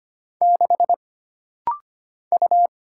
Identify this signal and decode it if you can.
Morse code, heard as a keyed tone, about 25 wpm, 700 Hz. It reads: 6 U
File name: Single Character (Letter:Number:Punctuation:Prosign) - Rapid-Fire - 25wpm